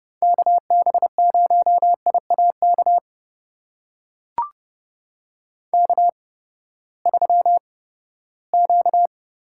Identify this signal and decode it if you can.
Morse code, heard as a keyed tone, about 30 wpm, spaced as for 17 wpm, 700 Hz. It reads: KB0IAK K3Q
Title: USA Call Signs - 8x Character Spacing - 30wpm